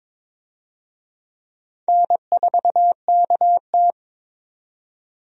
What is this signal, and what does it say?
Morse code, heard as a keyed tone, about 22 wpm, 700 Hz. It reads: N4KT
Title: USA Call Signs - 4 Characters - 22wpm